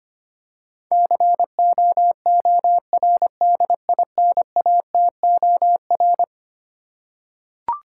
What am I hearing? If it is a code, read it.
Morse code, heard as a keyed tone, about 25 wpm, 700 Hz. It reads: COORDINATOR
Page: morse-code-ninja-practice-25wpm